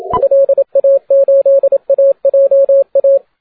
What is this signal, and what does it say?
Morse code, heard as a keyed tone, about 27 wpm, 560 Hz, with radio noise. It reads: LA8AJA
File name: 30m cw